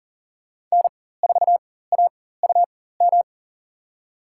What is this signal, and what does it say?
Morse code, heard as a keyed tone, about 40 wpm, spaced as for 19 wpm, 700 Hz. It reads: N4AUM
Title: USA Call Signs - 5 Characters - 4x Character Spacing - 40wpm